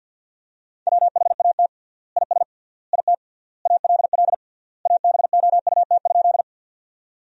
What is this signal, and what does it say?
Morse code, heard as a keyed tone, about 50 wpm, 700 Hz. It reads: WHAT IS IT ALL ABOUT?
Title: 3x Word-Spacing - Sentences from Top 100 Words - 50wpm